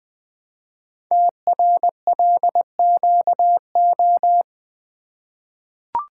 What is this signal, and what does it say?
Morse code, heard as a keyed tone, about 20 wpm, 700 Hz. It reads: TRLQO
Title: Code Groups - 5 Letter:Numbers - 20wpm